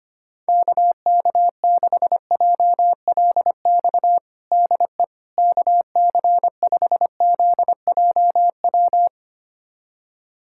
Morse code, heard as a keyed tone, about 25 wpm, 700 Hz. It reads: KK6JLX DE KC5ZJW
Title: USA Callsign DE Callsign 25wpm